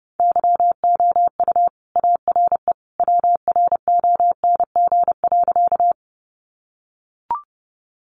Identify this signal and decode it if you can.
Morse code, heard as a keyed tone, about 30 wpm, 700 Hz. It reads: YOU ARE WRONG.
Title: Sentences from Top 900 Words 30wpm